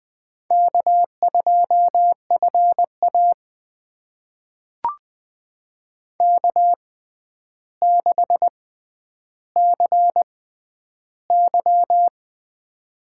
Morse code, heard as a keyed tone, about 19 wpm, spaced as for 10 wpm, 700 Hz. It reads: K2FA K6CY
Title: USA Call Signs - 4 Characters - 6x Character Spacing - 20wpm